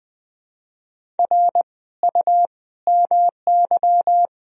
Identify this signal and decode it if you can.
Morse code, heard as a keyed tone, about 20 wpm, 700 Hz. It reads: R U MY
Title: Sentences from Top 300 Words and Abbreviations 20wpm